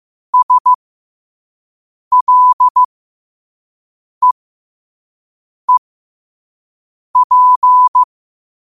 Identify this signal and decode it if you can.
Morse code, heard as a keyed tone, about 15 wpm, spaced as for 5 wpm, 1000 Hz. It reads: SLEEP
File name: qotdfiles / 5wpm